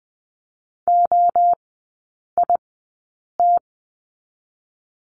Morse code, heard as a keyed tone, about 20 wpm, 700 Hz. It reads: O I T
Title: 029 Lesson 6e - Letter i - Three Characters 20wpm